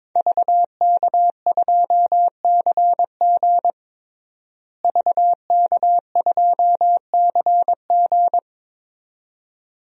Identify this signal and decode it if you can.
Morse code, heard as a keyed tone, about 22 wpm, 700 Hz. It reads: VK2CG VK2CG
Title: DXCC-100 Call Signs - 5 Characters - Repeated 2x - 22wpm